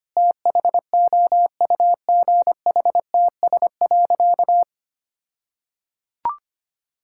Morse code, heard as a keyed tone, about 25 wpm, 700 Hz. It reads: THOUGHTS.
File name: practice / 25wpm / All-Age Appropriate Jokes - 25wpm